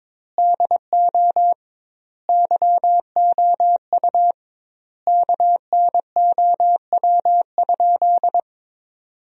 Morse code, heard as a keyed tone, about 22 wpm, 700 Hz. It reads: DO YOU KNOW?
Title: 2x Word-Spacing - Sentences from Top 100 Words - 22wpm